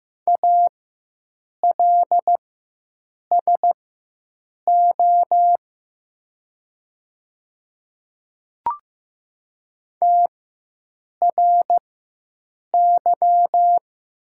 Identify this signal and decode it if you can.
Morse code, heard as a keyed tone, about 15 wpm, spaced as for 7 wpm, 690 Hz. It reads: ALSO TRY
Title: 4x Character Spacing - Top 500 Words - 15wpm